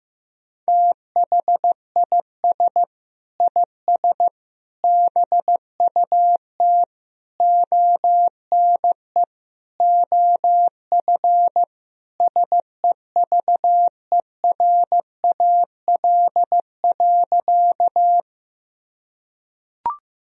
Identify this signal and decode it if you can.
Morse code, heard as a keyed tone, about 16 wpm, 700 Hz. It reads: THIS IS BUT ONE OF SEVERAL.
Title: Sentences from Top 1000 Words - Review Part 2 - 15wpm